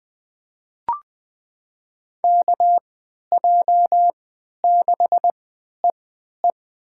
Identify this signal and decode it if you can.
Morse code, heard as a keyed tone, about 21 wpm, spaced as for 12 wpm, 700 Hz. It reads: KJ6EE
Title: USA Call Signs - 5 Characters - 3x Character Spacing - 20wpm